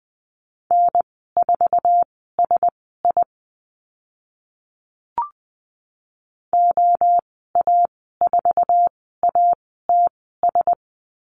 Morse code, heard as a keyed tone, about 20 wpm, spaced as for 14 wpm, 700 Hz. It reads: N4SI OA4ATS
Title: 055 Lesson 10g - Letter r - Call Signs 20wpm